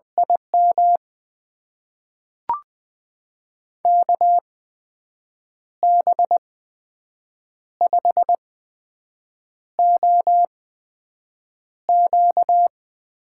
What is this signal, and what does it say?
Morse code, heard as a keyed tone, about 20 wpm, spaced as for 6 wpm, 700 Hz. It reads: IM KB5OQ
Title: USA Call Signs - 6 Characters - 8x Character Spacing - 20wpm